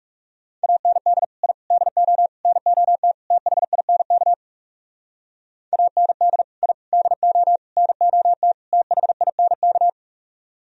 Morse code, heard as a keyed tone, about 36 wpm, 700 Hz. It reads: AND I DO NOT THINK AND I DO NOT THINK
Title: Speed-Racing - Sets of 5 Words 30wpm